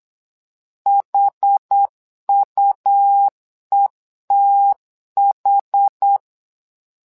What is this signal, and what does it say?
Morse code, heard as a keyed tone, about 8 wpm, 800 Hz. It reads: HUETH
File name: LEZIONI / 2 UHDN / 2 UHD 8wpm 2258